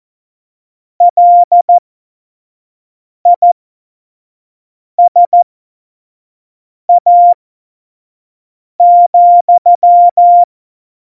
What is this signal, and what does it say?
Morse code, heard as a keyed tone, about 14 wpm, spaced as for 5 wpm, 700 Hz. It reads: LISA,
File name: Morsekurs / Witze / Witz 25. Juni 2019, 25 F70